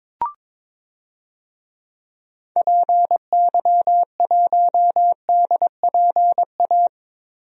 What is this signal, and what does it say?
Morse code, heard as a keyed tone, about 22 wpm, 700 Hz. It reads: PY1DPA
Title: DXCC-100 Call Signs - 6 Characters - Repeated 2x - 22wpm